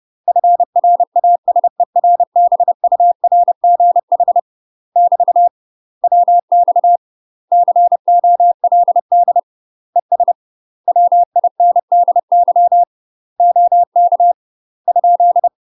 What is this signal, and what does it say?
Morse code, heard as a keyed tone, about 30 wpm, 700 Hz. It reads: FRASERBURGH <BT> WX COLD ES WINDY OK ?
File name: UK MASTER QSOs TRAINING / 30WPM / UK MASTER QSOs 0007